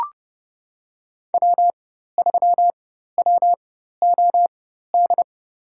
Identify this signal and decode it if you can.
Morse code, heard as a keyed tone, about 30 wpm, spaced as for 14 wpm, 700 Hz. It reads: W3WOD
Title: USA Call Signs - 5 Characters - 4x Character Spacing - 30wpm